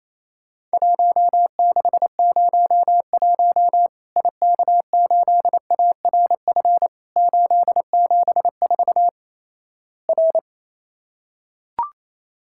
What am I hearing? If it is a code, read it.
Morse code, heard as a keyed tone, about 28 wpm, 700 Hz. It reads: 1601 IK8ARF 874 R
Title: Worked All Europe DX Contest - QTC Exchange - 28wpm